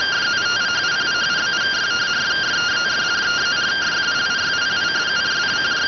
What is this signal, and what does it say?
Two-tone teleprinter signal, about 170 Hz shift.